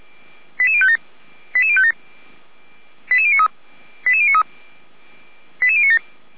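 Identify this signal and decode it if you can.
Five-tone selective call (ZVEI), heard as quick runs of five tones, tones of about 70 ms. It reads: 88057 88057 88093 88093 88097
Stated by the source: ZVEI five-tone call received by SDR, from the fire brigade feuerwehr-orlamuende.org (f85360).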